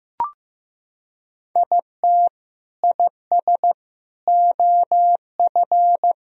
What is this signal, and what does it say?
Morse code, heard as a keyed tone, about 15 wpm, 700 Hz. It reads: IT IS OF
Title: Sentences from Top 1000 Words - Review Part 1 - 15wpm